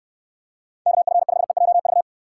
Morse code, heard as a keyed tone, about 69 wpm, 700 Hz. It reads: KF5EJV